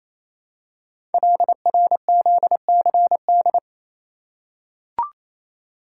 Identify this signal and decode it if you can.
Morse code, heard as a keyed tone, about 27 wpm, 700 Hz. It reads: LRZCD
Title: Code Groups - 5 Letters - 28wpm